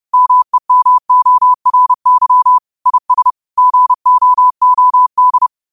Morse code, heard as a keyed tone, about 30 wpm, 1000 Hz. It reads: MEMORY IS GOOD
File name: qotdfiles / 30wpm